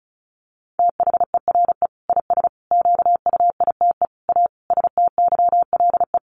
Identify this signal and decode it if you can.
Morse code, heard as a keyed tone, about 34 wpm, 700 Hz. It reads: THERE IS QUITE A STYLE
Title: Sentences from Top 800 Words 35wpm